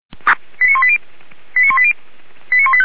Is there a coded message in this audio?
Yes: five-tone selective call (ZVEI) from 0.6 to 2.9 s, heard as quick runs of five tones, tones of about 70 ms. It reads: 89180 89180 89185